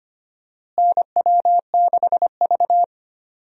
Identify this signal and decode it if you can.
Morse code, heard as a keyed tone, about 25 wpm, 700 Hz. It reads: NW6V
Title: USA Call Signs - 4 Characters - Repeated 3x - 25wpm